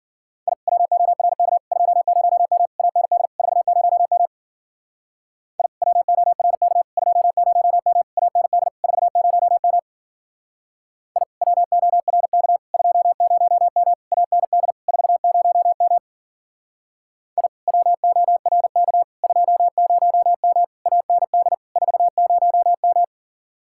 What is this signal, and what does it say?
Morse code, heard as a keyed tone, about 49 wpm, 700 Hz. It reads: I WORK 20M AND 40M I WORK 20M AND 40M I WORK 20M AND 40M I WORK 20M AND 40M
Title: Speed-Racing Common QSO Phrases Encore - 40wpm